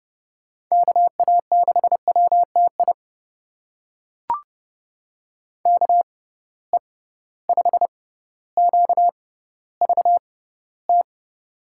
Morse code, heard as a keyed tone, about 30 wpm, spaced as for 15 wpm, 700 Hz. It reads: KA6WTI KE5QVT
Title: USA Call Signs - 6 Characters - 6x Character Spacing - 30wpm